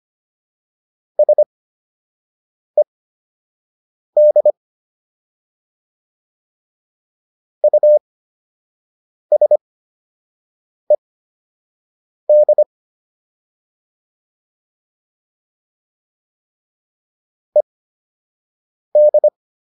Morse code, heard as a keyed tone, about 25 wpm, spaced as for 6 wpm, 600 Hz. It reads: SED USED ED